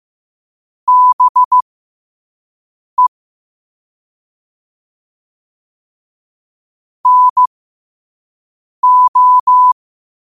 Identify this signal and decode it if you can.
Morse code, heard as a keyed tone, about 15 wpm, spaced as for 5 wpm, 1000 Hz. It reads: BE NO
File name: qotdfiles / 5wpm